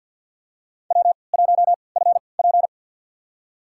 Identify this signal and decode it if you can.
Morse code, heard as a keyed tone, about 52 wpm, spaced as for 29 wpm, 700 Hz. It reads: W1FP
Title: USA Call Signs - 4 Characters - 3x Character Spacing - 50wpm